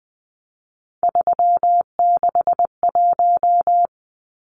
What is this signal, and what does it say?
Morse code, heard as a keyed tone, about 20 wpm, 700 Hz. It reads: 361
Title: Numbers - 3-Digit 20wpm